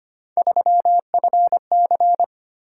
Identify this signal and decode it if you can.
Morse code, heard as a keyed tone, about 25 wpm, 700 Hz. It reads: 3FC
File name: Three Letter-Number Combinations - 25wpm